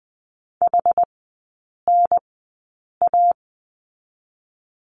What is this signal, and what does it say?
Morse code, heard as a keyed tone, about 20 wpm, spaced as for 8 wpm, 690 Hz. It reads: HNA
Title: 081 Lesson 14e - Number 2 - Three Characters 20wpm